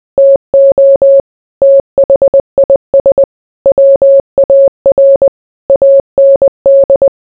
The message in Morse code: TO THIS WAR AND

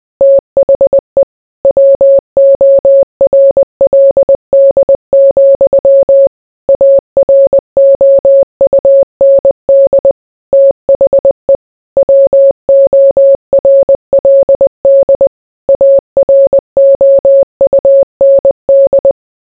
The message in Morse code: THE WORLD, AROUND THE WORLD AROUND

